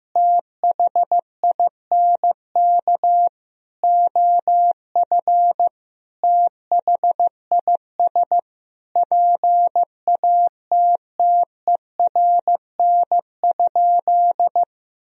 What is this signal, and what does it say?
Morse code, heard as a keyed tone, about 15 wpm, 700 Hz. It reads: THINK OF THIS PATTERN?
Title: Sentences from Top 1000 Words - Review Part 2 - 15wpm